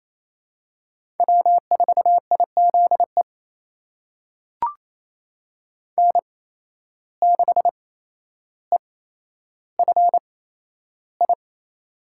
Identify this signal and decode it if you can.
Morse code, heard as a keyed tone, about 28 wpm, spaced as for 12 wpm, 700 Hz. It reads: W4IZE N6EFI
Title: USA Call Signs - 5 Characters - 8x Character Spacing - 28wpm